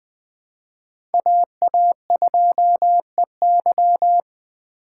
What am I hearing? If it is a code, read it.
Morse code, heard as a keyed tone, about 20 wpm, 700 Hz. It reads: AA2EY
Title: Speed Racing - USA 2x2 Call Signs 17wpm